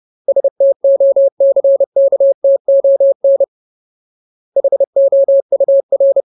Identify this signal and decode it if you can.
Morse code, heard as a keyed tone, about 30 wpm, 550 Hz. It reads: STOCKTON HOUR
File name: Ultimate Word List / 30WPM Ultimate